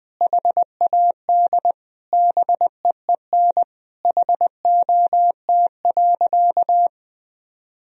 Morse code, heard as a keyed tone, about 20 wpm, 700 Hz. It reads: HAD BEEN HOT.
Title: Sentences from Top 1000 Words - Review Part 3 - 20wpm